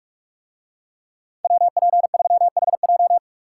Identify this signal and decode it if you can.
Morse code, heard as a keyed tone, about 42 wpm, 700 Hz. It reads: WP3HJ